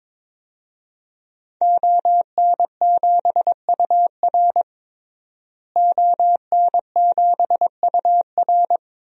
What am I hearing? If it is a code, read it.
Morse code, heard as a keyed tone, about 22 wpm, 700 Hz. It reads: ON7UR ON7UR
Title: DXCC-100 Call Signs - 5 Characters - Repeated 3x - 22wpm